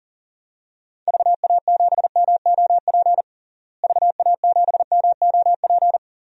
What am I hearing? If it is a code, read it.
Morse code, heard as a keyed tone, about 40 wpm, 700 Hz. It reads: VA7MOP VA7MOP
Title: DXCC-100 Call Signs - 6 Characters - Repeated 3x - 40wpm